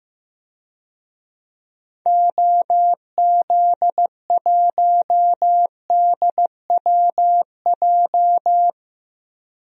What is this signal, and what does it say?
Morse code, heard as a keyed tone, about 15 wpm, 700 Hz. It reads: OZ1DWJ
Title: DXCC-100 Call Signs - 6 Characters - 15wpm